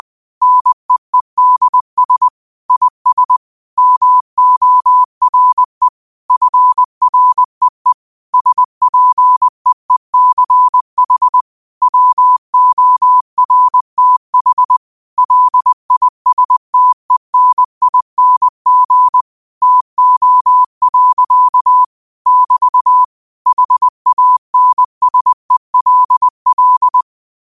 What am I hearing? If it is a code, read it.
Morse code, heard as a keyed tone, about 20 wpm, 1000 Hz. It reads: NEEDS IS MORE FREE SPEECH WORTH LISTENING TO. <BT> HANSELL